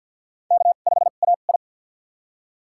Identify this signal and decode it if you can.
Morse code, heard as a keyed tone, about 51 wpm, spaced as for 36 wpm, 700 Hz. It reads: K5AI
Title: USA Call Signs - 4 Characters - 2x Character Spacing - 50wpm